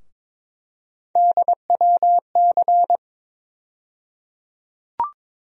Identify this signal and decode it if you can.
Morse code, heard as a keyed tone, about 22 wpm, 700 Hz. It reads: DWC
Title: USA Call Sign Postfixes - 3 Letters 22wpm